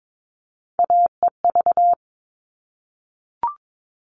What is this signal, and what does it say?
Morse code, heard as a keyed tone, about 22 wpm, 700 Hz. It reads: AEV
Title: All Cut Numbers - 3-Digit 22wpm